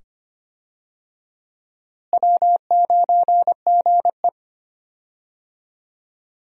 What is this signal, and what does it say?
Morse code, heard as a keyed tone, about 25 wpm, 700 Hz. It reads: W9GE
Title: USA Call Signs - 4 Characters - Repeated 2x - 25wpm